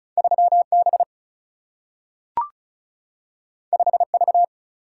Morse code, heard as a keyed tone, about 36 wpm, 700 Hz. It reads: 3B 5V
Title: Single-Letter Single-Number 35wpm